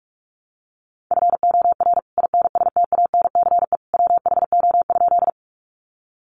Morse code, heard as a keyed tone, about 44 wpm, 700 Hz. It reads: FOR INSTANCE WHO?
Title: Sentences from Top 800 Words 45wpm